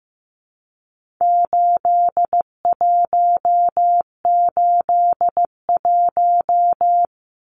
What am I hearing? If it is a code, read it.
Morse code, heard as a keyed tone, about 15 wpm, 700 Hz. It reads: 8181
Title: Numbers - 4-Digit 15wpm